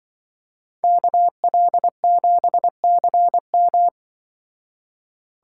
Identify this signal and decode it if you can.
Morse code, heard as a keyed tone, about 24 wpm, 700 Hz. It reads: KL7CM
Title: Speed Racing - USA 2x2 Call Signs 20wpm